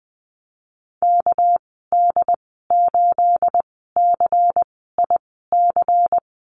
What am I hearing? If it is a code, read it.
Morse code, heard as a keyed tone, about 19 wpm, spaced as for 14 wpm, 700 Hz. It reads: KD8CIC
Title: 245 Lesson 39g - Letter z - Call Signs 20wpm